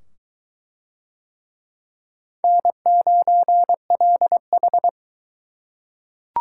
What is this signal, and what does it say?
Morse code, heard as a keyed tone, about 20 wpm, 700 Hz. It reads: N9LH